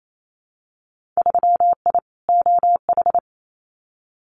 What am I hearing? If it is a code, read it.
Morse code, heard as a keyed tone, about 28 wpm, 700 Hz. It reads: 3I OH
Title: Winter Field Day Exchanges Encore 28wpm